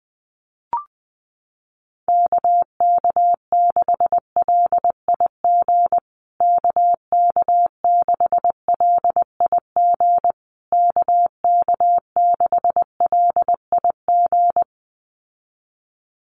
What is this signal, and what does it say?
Morse code, heard as a keyed tone, about 20 wpm, 700 Hz. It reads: KK6LIG KK6LIG KK6LIG
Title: USA Call Signs Repeated 3x 20wpm